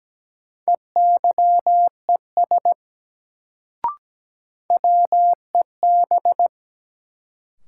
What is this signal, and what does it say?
Morse code, heard as a keyed tone, about 17 wpm, 700 Hz. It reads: EYES WEB
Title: Sentences from Top 1000 Words - Review Part 4 - New Word Drill - 17wpm